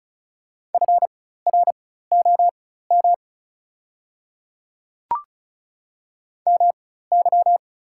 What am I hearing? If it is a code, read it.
Morse code, heard as a keyed tone, about 36 wpm, spaced as for 16 wpm, 700 Hz. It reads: FROM MY
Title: 4x Character Spacing - Top 100 Words - 35wpm